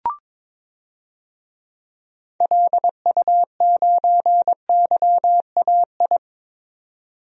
Morse code, heard as a keyed tone, about 22 wpm, 700 Hz. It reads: LU9YAI
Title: DXCC-100 Call Signs - 6 Characters - 22wpm